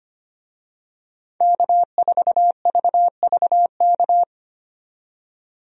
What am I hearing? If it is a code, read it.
Morse code, heard as a keyed tone, about 25 wpm, 700 Hz. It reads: K4VVK
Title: USA Call Signs - 5 Characters - 25wpm